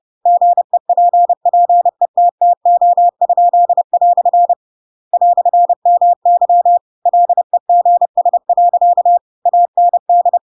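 Morse code, heard as a keyed tone, about 30 wpm, 700 Hz. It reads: GEPPETTO?" "MY LEGS. AND
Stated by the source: Chapter 2 of The Adventures of Pinocchio read as a book in Morse code (CW) at 30 wpm, by ik1hge.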